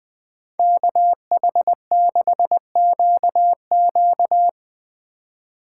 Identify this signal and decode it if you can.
Morse code, heard as a keyed tone, about 23 wpm, 700 Hz. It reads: KH6QQ
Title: Speed Racing - USA 2x2 Call Signs 20wpm